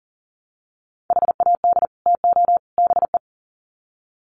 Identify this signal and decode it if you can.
Morse code, heard as a keyed tone, about 40 wpm, 700 Hz. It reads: HAD TO BE
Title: Sets of 3 Words 40wpm